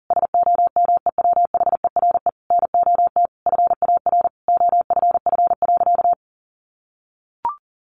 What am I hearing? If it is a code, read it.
Morse code, heard as a keyed tone, about 40 wpm, 700 Hz. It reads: SOMEWHERE NOT FAR OFF.
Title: Sentences from Top 1400 Words 40wpm